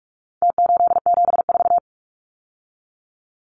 Morse code, heard as a keyed tone, about 44 wpm, 700 Hz. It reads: T874
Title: Numbers - Contest Style - 4-Digit 45wpm